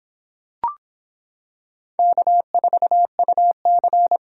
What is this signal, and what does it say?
Morse code, heard as a keyed tone, about 26 wpm, 700 Hz. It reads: K4UC